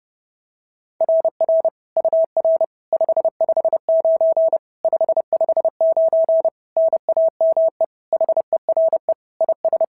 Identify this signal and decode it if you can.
Morse code, heard as a keyed tone, about 30 wpm, 660 Hz. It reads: RR UR 559 559 NAME HERE IS
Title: SOTA QSO Practice 30wpm